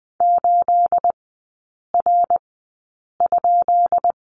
Morse code, heard as a keyed tone, about 20 wpm, spaced as for 8 wpm, 690 Hz. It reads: 8R?
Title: 212 Lesson 34e - Number 8 - Three Characters 20wpm